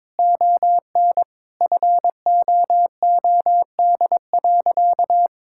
Morse code, heard as a keyed tone, about 22 wpm, 700 Hz. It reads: ON FOOD.
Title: Sentences from Top 1000 Words - Review Part 5 - 22wpm